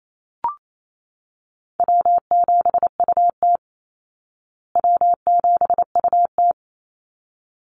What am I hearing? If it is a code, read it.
Morse code, heard as a keyed tone, about 28 wpm, 700 Hz. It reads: W7UT W7UT
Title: Easy Call Signs Repeated 2x - 28wpm